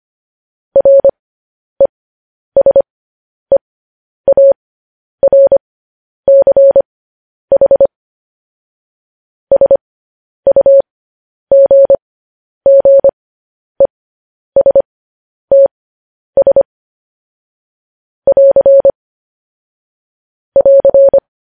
Morse code, heard as a keyed tone, about 25 wpm, spaced as for 10 wpm, 550 Hz. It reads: RESEARCH SUGGESTS <AR> <AR>